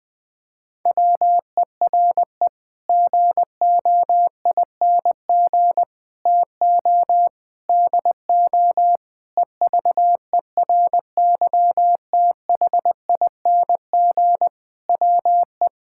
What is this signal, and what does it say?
Morse code, heard as a keyed tone, about 20 wpm, 700 Hz. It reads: WERE GOING TO DO EVERYTHING WE
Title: Sets of 8 Words - 20wpm